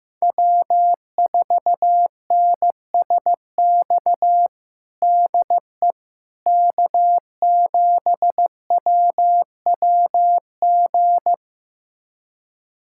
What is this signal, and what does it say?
Morse code, heard as a keyed tone, about 15 wpm, 700 Hz. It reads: W4NSX DE K7WWG
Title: USA Callsign DE Callsign 15wpm